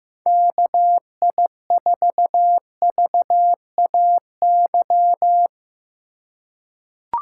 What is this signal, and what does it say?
Morse code, heard as a keyed tone, about 15 wpm, 700 Hz. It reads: KI4VAY
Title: USA Call Signs - 3x Character Spacing - 15wpm